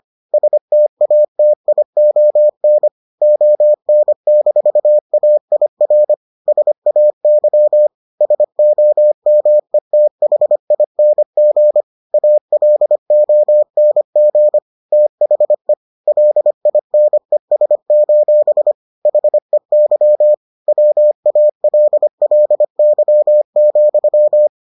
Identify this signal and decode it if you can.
Morse code, heard as a keyed tone, about 25 wpm, 600 Hz. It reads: STATION ON-AIR SAY SOMETHING ALONG THE LINES: HEY WALLY,